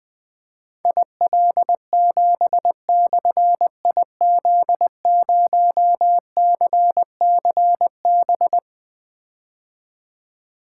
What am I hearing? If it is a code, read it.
Morse code, heard as a keyed tone, about 20 wpm, 700 Hz. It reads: IL7/IZ0CCB
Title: Difficult Call Signs 20wpm